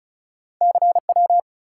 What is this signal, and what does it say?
Morse code, heard as a keyed tone, about 36 wpm, 700 Hz. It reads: CW
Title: Top 180 QSO Elements 35wpm